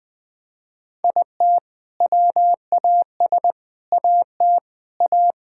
Morse code, heard as a keyed tone, about 20 wpm, 700 Hz. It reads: IT WAS AT A
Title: Sentences from Top 1600 Words 20wpm